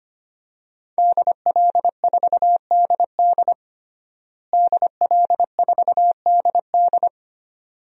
Morse code, heard as a keyed tone, about 25 wpm, 700 Hz. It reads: DL4DD DL4DD
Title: DXCC-100 Call Signs - 5 Characters - Repeated 2x - 25wpm